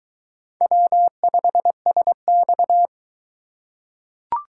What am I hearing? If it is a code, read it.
Morse code, heard as a keyed tone, about 20 wpm, 700 Hz. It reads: W5SX